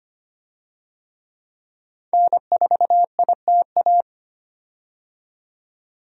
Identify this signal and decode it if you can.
Morse code, heard as a keyed tone, about 25 wpm, 700 Hz. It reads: N4ITA
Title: USA Call Signs - 5 Characters - 25wpm